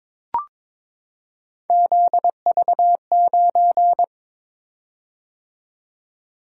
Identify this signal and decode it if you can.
Morse code, heard as a keyed tone, about 22 wpm, 700 Hz. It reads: ZV9
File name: practice / 22wpm / Three Letter-Number Combinations - 22wpm